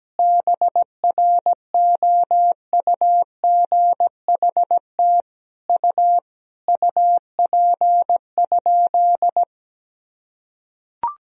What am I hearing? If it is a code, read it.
Morse code, heard as a keyed tone, about 17 wpm, 700 Hz. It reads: BROUGHT U UP?
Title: Sentences from Top 500 Words and Abbreviations 17wpm